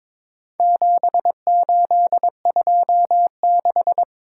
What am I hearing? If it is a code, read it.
Morse code, heard as a keyed tone, about 22 wpm, 700 Hz. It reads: 7826
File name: practice / 22wpm / Four Letter-Number Combinations - 22wpm